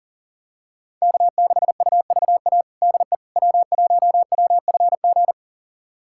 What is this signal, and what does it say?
Morse code, heard as a keyed tone, about 40 wpm, 700 Hz. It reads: K6UVA DE W1WFG
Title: USA Callsign DE Callsign - 5 Characters - 40wpm